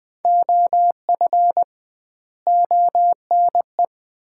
Morse code, heard as a keyed tone, about 20 wpm, 700 Hz. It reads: OF ONE
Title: 2x Word-Spacing - Sets of 4 Words - 20wpm